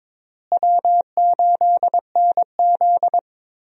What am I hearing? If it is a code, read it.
Morse code, heard as a keyed tone, about 22 wpm, 700 Hz. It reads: W8NZ
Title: USA Call Signs - 4 Characters - Repeated 3x - 22wpm